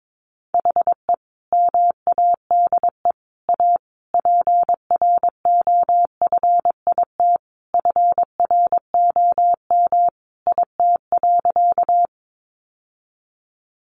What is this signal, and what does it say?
Morse code, heard as a keyed tone, about 22 wpm, 700 Hz. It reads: HE MADE A PROFIT FROM IT.
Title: Sentences from Top 800 Words 22wpm